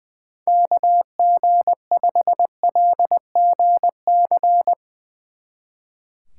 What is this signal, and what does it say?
Morse code, heard as a keyed tone, about 20 wpm, 700 Hz. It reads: KG5LGC
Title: Speed Racing - USA 2x3 Call Signs 20wpm